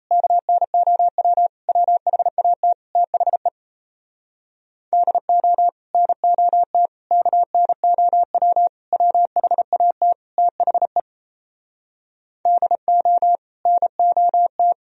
Morse code, heard as a keyed tone, about 37 wpm, 700 Hz. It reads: KNOW WHAT THE DO NOT KNOW WHAT THE DO NOT